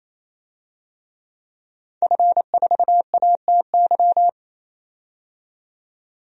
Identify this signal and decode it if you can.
Morse code, heard as a keyed tone, about 28 wpm, 700 Hz. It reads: F4ATY
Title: DXCC-100 Call Signs - 5 Characters - 28wpm